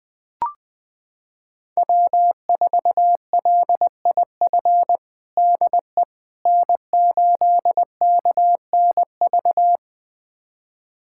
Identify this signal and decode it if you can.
Morse code, heard as a keyed tone, about 20 wpm, 700 Hz. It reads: W4LIF DE N8KNV
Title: USA Callsign DE Callsign - 5 Characters - 20wpm